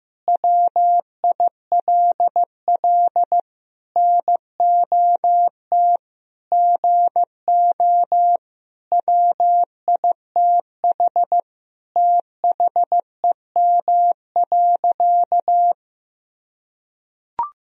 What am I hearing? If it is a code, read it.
Morse code, heard as a keyed tone, about 15 wpm, 700 Hz. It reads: WILL NOT GO WITH THEM.
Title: Sentences from Top 1000 Words - Review Part 4 - 15wpm